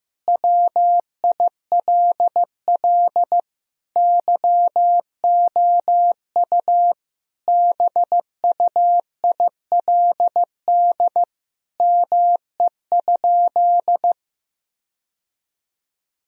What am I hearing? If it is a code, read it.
Morse code, heard as a keyed tone, about 15 wpm, 700 Hz. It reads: WILL YOU BUILD ME?
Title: Sentences from Top 1000 Words - Review Part 1 - 15wpm